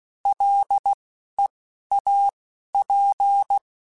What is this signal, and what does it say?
Morse code, heard as a keyed tone, about 16 wpm, spaced as for 12 wpm, 800 Hz. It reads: LEAP